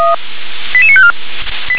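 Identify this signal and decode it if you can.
Five-tone selective call (ZVEI), heard as quick runs of five tones, tones of about 70 ms. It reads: 88954
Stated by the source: ZVEI five-tone selective call received by SDR.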